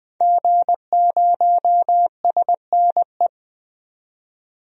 Morse code, heard as a keyed tone, about 21 wpm, 700 Hz. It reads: G0SNE
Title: DXCC-100 Call Signs - 5 Characters - 20wpm